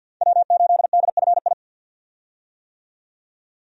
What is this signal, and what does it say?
Morse code, heard as a keyed tone, about 50 wpm, 700 Hz. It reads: W8DFI
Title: USA Call Signs - 5 Characters - 50wpm